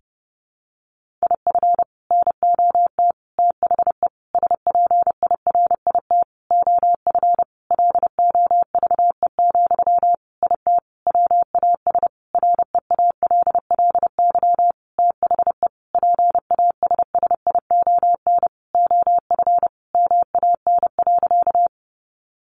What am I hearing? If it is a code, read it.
Morse code, heard as a keyed tone, about 30 wpm, 700 Hz. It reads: IF NOT THE SPIRIT OF LOVE, IT WAS REALLY THE PASSION OF MAN.
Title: Sentences from Top 1400 Words 30wpm